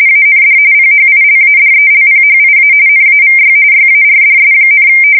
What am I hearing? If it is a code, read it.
RTTY, about 45.45 baud. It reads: AND PHILADELPHIA NAVY YARDS,
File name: fsk170 / USS-New-Jersey